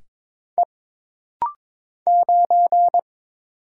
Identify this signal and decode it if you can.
Morse code, heard as a keyed tone, about 22 wpm, 700 Hz. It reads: E 9
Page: This rapid-fire timing is geared to push your ICR (Instant Character Recognition) skills.